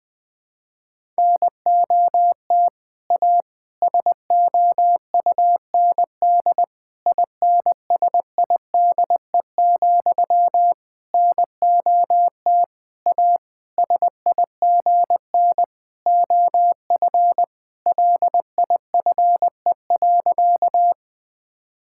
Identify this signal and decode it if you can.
Morse code, heard as a keyed tone, about 20 wpm, 700 Hz. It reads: NOT A SOUND INSIDE, NOT A SIGN OF LIFE.
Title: Sentences from Top 1000 Words - Review Part 6 - 20wpm